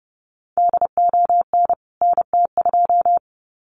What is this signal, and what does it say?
Morse code, heard as a keyed tone, about 30 wpm, 700 Hz. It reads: DON NT2
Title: CWT Exchange 30wpm